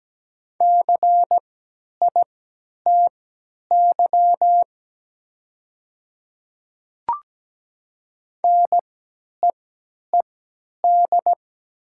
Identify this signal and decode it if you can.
Morse code, heard as a keyed tone, about 17 wpm, spaced as for 10 wpm, 700 Hz. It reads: CITY NEED
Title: 3x Character Spacing - Top 300 Words - 17wpm